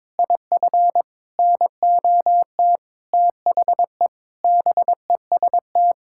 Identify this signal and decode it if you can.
Morse code, heard as a keyed tone, about 22 wpm, 700 Hz. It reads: IF NOT THE BEST
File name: practice / 22wpm / Sets of 8 Words - 22wpm